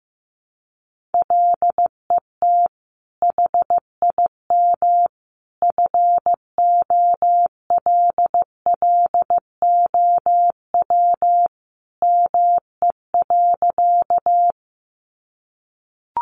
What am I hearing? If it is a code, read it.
Morse code, heard as a keyed tone, about 14 wpm, 700 Hz. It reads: LET HIM FOLLOW ME.
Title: Sentences from Top 300 Words 15wpm